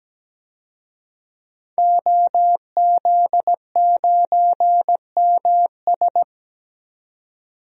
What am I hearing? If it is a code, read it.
Morse code, heard as a keyed tone, about 17 wpm, 700 Hz. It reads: OZ9MS
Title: DXCC-100 Call Signs - 5 Characters - 17wpm